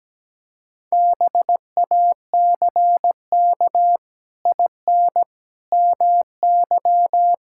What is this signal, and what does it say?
Morse code, heard as a keyed tone, about 17 wpm, 700 Hz. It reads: BACK IN MY
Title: Pangram Challenge v2 - 17wpm